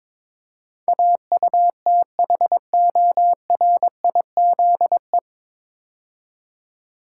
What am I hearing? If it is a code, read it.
Morse code, heard as a keyed tone, about 21 wpm, 700 Hz. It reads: AUTHORIZE
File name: 9 Letter Words - First 3 Letters Given - 22wpm